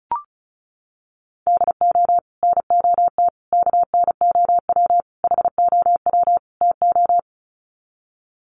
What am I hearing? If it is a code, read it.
Morse code, heard as a keyed tone, about 34 wpm, 700 Hz. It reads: DO NOT KNOW HOW TO
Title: Sets of 5 Words 35wpm